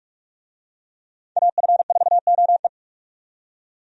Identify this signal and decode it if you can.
Morse code, heard as a keyed tone, about 43 wpm, 700 Hz. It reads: AF4OE
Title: USA Call Signs - 5 Characters - 45wpm